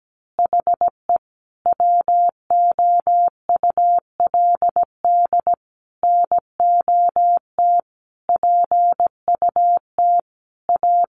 Morse code, heard as a keyed tone, about 17 wpm, 700 Hz. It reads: HE WOULD NOT PUT A
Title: Easy Phrases 17wpm